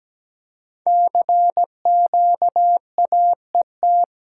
Morse code, heard as a keyed tone, about 17 wpm, 700 Hz. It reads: CQAET
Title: Code Groups - 5 Letter:Numbers - 17wpm